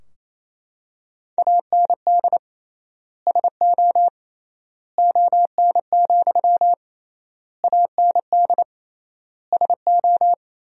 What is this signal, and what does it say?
Morse code, heard as a keyed tone, about 28 wpm, 700 Hz. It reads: AND SO ON, AND SO
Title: 3x Word-Spacing - Sentences from Top 100 Words - 28wpm